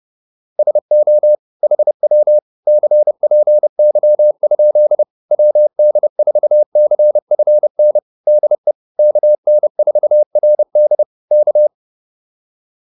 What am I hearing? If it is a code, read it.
Morse code, heard as a keyed tone, about 30 wpm, 600 Hz. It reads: SO HW CPY? WD4CFN DE KN4RD K